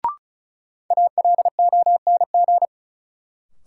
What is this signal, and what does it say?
Morse code, heard as a keyed tone, about 36 wpm, 700 Hz. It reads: ALONG
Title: Sentences from Top 700 Words - New Word Drill - 35wpm